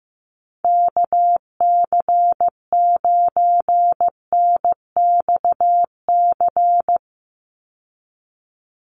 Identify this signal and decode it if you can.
Morse code, heard as a keyed tone, about 15 wpm, 700 Hz. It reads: KC9NXC
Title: USA Call Signs 15wpm